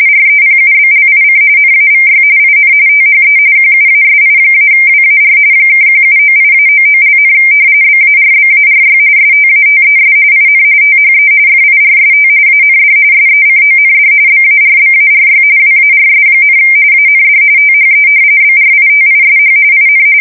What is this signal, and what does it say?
RTTY, about 45.45 baud. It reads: REPUBLIC NAVY IN THE QUASI-WAR (1798-99), BARBARY STATES IN THE FIRST AND SECOND BARBARY WARS, AND THE BRITISH IN
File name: fsk170 / usnavy1